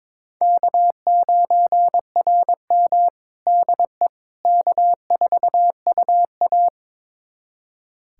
Morse code, heard as a keyed tone, about 22 wpm, 700 Hz. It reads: K9RM DE K4UA
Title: USA Callsign DE Callsign - 4 Characters - 22wpm